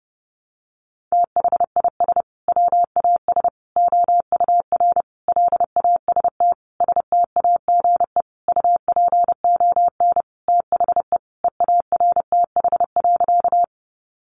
Morse code, heard as a keyed tone, about 30 wpm, 700 Hz. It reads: THIS WAS OUR LAST STAGE UPON THE EARTH.
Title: Sentences from Top 1200 Words 30wpm